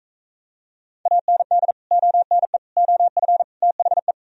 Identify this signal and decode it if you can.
Morse code, heard as a keyed tone, about 45 wpm, 690 Hz. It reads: AND ONE OF THE